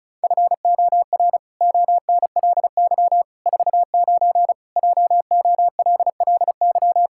Morse code, heard as a keyed tone, about 36 wpm, 700 Hz. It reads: FOR ONLY 49 JOLLY
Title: Pangram Challenge v2 - 35wpm